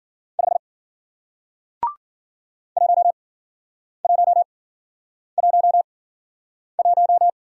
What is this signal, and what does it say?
Morse code, heard as a keyed tone, about 55 wpm, spaced as for 9 wpm, 700 Hz. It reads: 5 1111